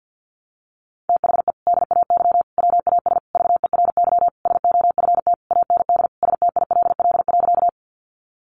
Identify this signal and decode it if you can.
Morse code, heard as a keyed tone, about 50 wpm, 700 Hz. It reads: THE DAY WAS VERY SOFT AND STILL.
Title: Sentences from Top 1500 Words 50wpm